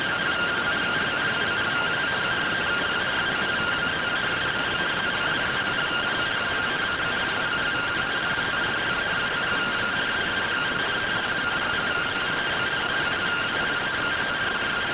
Two-tone teleprinter signal, about 170 Hz shift.